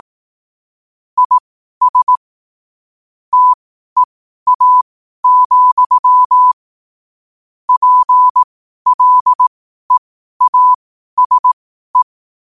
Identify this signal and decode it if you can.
Morse code, heard as a keyed tone, about 18 wpm, spaced as for 12 wpm, 1000 Hz. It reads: IS TEA, PLEASE
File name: qotdfiles / 12wpm